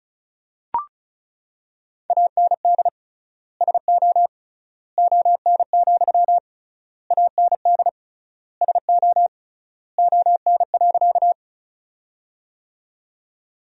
Morse code, heard as a keyed tone, about 36 wpm, 700 Hz. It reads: AND SO ON, AND SO ON.
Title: 3x Word-Spacing - Sentences from Top 100 Words - 35wpm